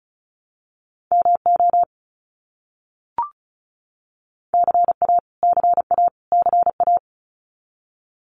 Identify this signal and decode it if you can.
Morse code, heard as a keyed tone, about 34 wpm, 700 Hz. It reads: MO CA CA CA
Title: US States Repeated 3x 35wpm